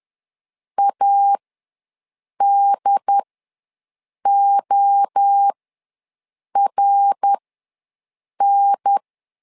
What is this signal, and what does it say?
Morse code, heard as a keyed tone, about 10 wpm, spaced as for 6 wpm, 800 Hz. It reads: ADORN
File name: LEZIONI / 3 MWRO / 3 MWRO ws 10wpm 2310